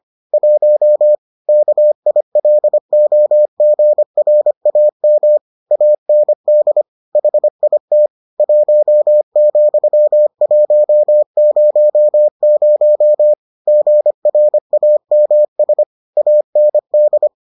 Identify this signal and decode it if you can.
Morse code, heard as a keyed tone, about 25 wpm, 600 Hz. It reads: 1 KILOGRAM AND HIT 1,100 GRAMS AND